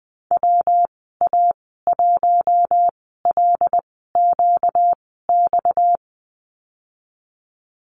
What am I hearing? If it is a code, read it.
Morse code, heard as a keyed tone, about 20 wpm, spaced as for 15 wpm, 700 Hz. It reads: WA1LQX
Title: 245 Lesson 39g - Letter z - Call Signs 20wpm